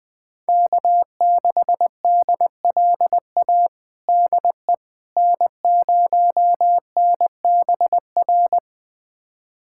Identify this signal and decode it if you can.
Morse code, heard as a keyed tone, about 21 wpm, 700 Hz. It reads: K6DLA DE N0NBR
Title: USA Callsign DE Callsign - 5 Characters - 20wpm